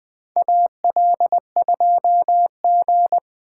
Morse code, heard as a keyed tone, about 20 wpm, 700 Hz. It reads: AL2G
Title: USA Call Signs - 4 Characters - 4x Character Spacing - 20wpm